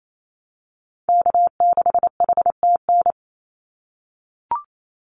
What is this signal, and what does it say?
Morse code, heard as a keyed tone, about 28 wpm, 700 Hz. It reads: K6HTN
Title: Call Signs - 28wpm